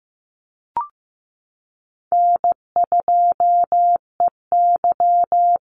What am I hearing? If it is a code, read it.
Morse code, heard as a keyed tone, about 15 wpm, 700 Hz. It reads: N2EY
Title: Call Signs Repeated 2x - 15wpm